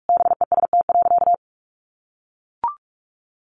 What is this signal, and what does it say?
Morse code, heard as a keyed tone, about 44 wpm, 700 Hz. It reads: BEST.
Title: Sentences from Top 600 Words 45wpm